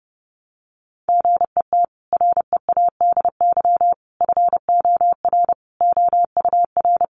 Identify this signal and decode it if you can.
Morse code, heard as a keyed tone, about 30 wpm, 700 Hz. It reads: GET READY FOR OUR
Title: Sentences from Top 1200 Words 30wpm